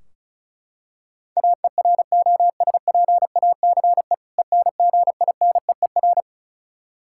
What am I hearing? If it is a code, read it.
Morse code, heard as a keyed tone, about 36 wpm, 700 Hz. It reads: AEROSPACE ENGINEER
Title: Practice QSO - Exchange 3 - More Info - 35wpm